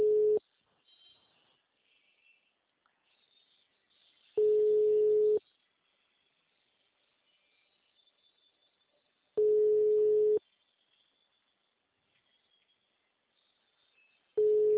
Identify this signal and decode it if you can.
Ringback tone (Europe, 425 Hz).